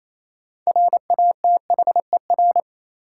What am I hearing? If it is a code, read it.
Morse code, heard as a keyed tone, about 28 wpm, 700 Hz. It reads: RATHER
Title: Sentences from Top 700 Words - New Word Drill - 28wpm